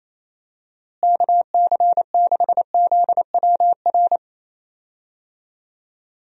Morse code, heard as a keyed tone, about 28 wpm, 700 Hz. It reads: KC6ZWR
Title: USA Call Signs - 6 Characters - Repeated 3x - 28wpm